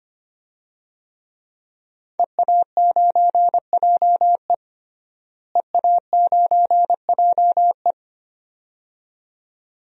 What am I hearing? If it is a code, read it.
Morse code, heard as a keyed tone, about 25 wpm, 700 Hz. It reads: EA9JE EA9JE
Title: DXCC-100 Call Signs - 5 Characters - Repeated 2x - 25wpm